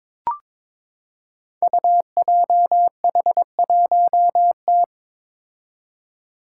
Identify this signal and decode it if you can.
Morse code, heard as a keyed tone, about 22 wpm, 700 Hz. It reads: UJH1T
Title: Five Challenging Letter-Number Combinations 22wpm